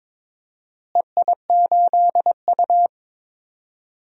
Morse code, heard as a keyed tone, about 22 wpm, 700 Hz. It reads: EI8U
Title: DXCC-100 Call Signs - 4 Characters - Repeated 2x - 22wpm